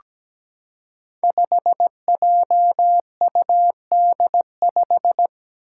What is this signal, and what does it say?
Morse code, heard as a keyed tone, about 17 wpm, 700 Hz. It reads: 5JUD5
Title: Five Challenging Letter-Number Combinations 17wpm